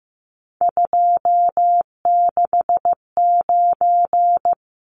Morse code, heard as a keyed tone, about 15 wpm, 700 Hz. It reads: 269
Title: Numbers - 3-Digit 15wpm